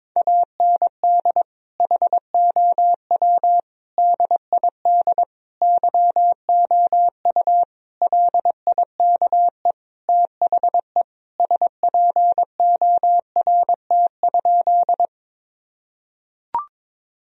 Morse code, heard as a keyed tone, about 22 wpm, 700 Hz. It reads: AND HOW DID YOU LIKE THE SPORT?
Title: Sentences from Top 1000 Words - Review Part 4 - 22wpm